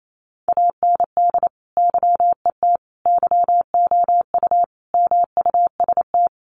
Morse code, heard as a keyed tone, about 28 wpm, 700 Hz. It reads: AND YET YOU MUST
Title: Sentences from Top 1300 Words 28wpm